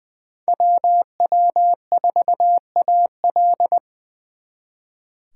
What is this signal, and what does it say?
Morse code, heard as a keyed tone, about 23 wpm, 700 Hz. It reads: WW4AL
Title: Speed Racing - USA 2x2 Call Signs 20wpm